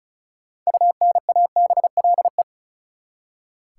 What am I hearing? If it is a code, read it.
Morse code, heard as a keyed tone, about 36 wpm, 700 Hz. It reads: UNABLE
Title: Six Letter Words - 35wpm